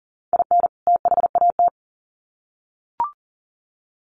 Morse code, heard as a keyed tone, about 40 wpm, 700 Hz. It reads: IN THAT
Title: Sets of 2 Words 40wpm